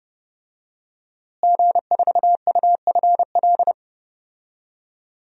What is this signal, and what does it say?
Morse code, heard as a keyed tone, about 30 wpm, 700 Hz. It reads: G4UFL
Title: DXCC-100 Call Signs - 5 Characters - Repeated 3x - 30wpm